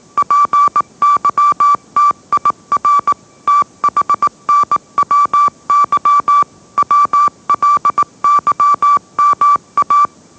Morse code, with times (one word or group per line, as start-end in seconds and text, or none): 0.2-3.1 PYTIR
3.5-6.4 THNWY
6.8-10.1 WLYMA